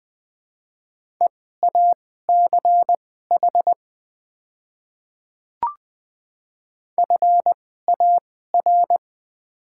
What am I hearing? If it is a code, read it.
Morse code, heard as a keyed tone, about 20 wpm, spaced as for 14 wpm, 690 Hz. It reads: EACH FAR
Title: 2x Character Spacing - Top 300 Words - 20wpm